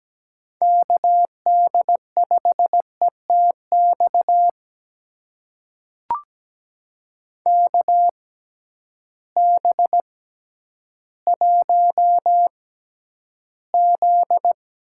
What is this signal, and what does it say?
Morse code, heard as a keyed tone, about 17 wpm, spaced as for 10 wpm, 700 Hz. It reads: KD5ETX KB1Z
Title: USA Call Signs - 6 Characters - 6x Character Spacing - 17wpm